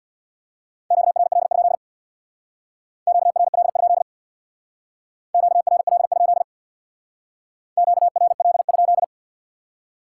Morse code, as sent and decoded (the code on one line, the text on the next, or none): --.- .-. .-.. ..--.. / --.- .-. .-.. ..--.. / --.- .-. .-.. ..--.. / --.- .-. .-.. ..--..
QRL? QRL? QRL? QRL?